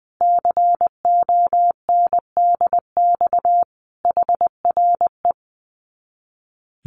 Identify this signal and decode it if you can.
Morse code, heard as a keyed tone, about 20 wpm, 700 Hz. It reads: CONDX HRE
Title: Common QSO Phrases 20wpm